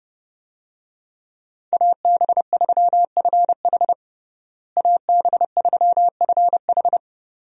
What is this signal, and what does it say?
Morse code, heard as a keyed tone, about 29 wpm, 700 Hz. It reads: AB3FH AB3FH
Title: USA Call Signs - 5 Characters - Repeated 2x - 30wpm